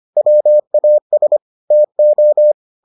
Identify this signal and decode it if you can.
Morse code, heard as a keyed tone, about 25 wpm, 600 Hz. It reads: WAS TO